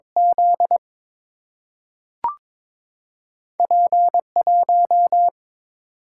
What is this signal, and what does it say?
Morse code, heard as a keyed tone, about 22 wpm, 700 Hz. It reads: Z P1
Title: Single-Letter Single-Number 22wpm